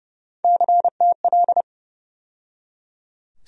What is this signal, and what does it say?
Morse code, heard as a keyed tone, about 32 wpm, 700 Hz. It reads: CTL
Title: USA Call Sign Postfixes - 3 Letters 30wpm